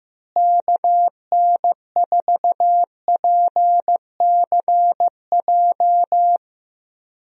Morse code, heard as a keyed tone, about 16 wpm, 700 Hz. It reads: KN4PCJ
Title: USA Call Signs - 6 Characters - Repeated 3x - 15wpm